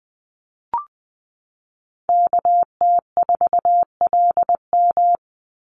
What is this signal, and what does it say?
Morse code, heard as a keyed tone, about 20 wpm, 700 Hz. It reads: KT4LM
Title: USA Call Signs 20wpm